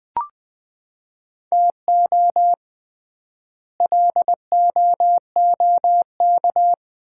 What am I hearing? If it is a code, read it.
Morse code, heard as a keyed tone, about 20 wpm, 700 Hz. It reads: TO LOOK
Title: 3x Word-Spacing - Sets of 2 Words - 20wpm